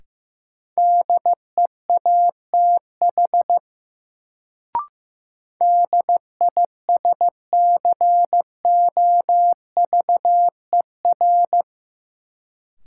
Morse code, heard as a keyed tone, about 15 wpm, 700 Hz. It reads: DEATH DISCOVER
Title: Sentences from Top 1100 Words - New Word Drill Alt - 15wpm